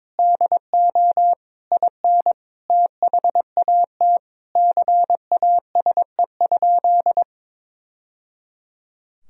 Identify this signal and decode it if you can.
Morse code, heard as a keyed tone, about 22 wpm, 700 Hz. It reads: DO IN THAT CASE?
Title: Sentences from Top 1000 Words - Review Part 3 - 22wpm